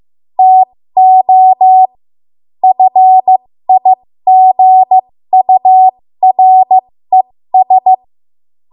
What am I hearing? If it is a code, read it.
Morse code, heard as a keyed tone, about 15 wpm, spaced as for 13 wpm, 750 Hz. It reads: TO FIGURES
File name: Morse / 13WPM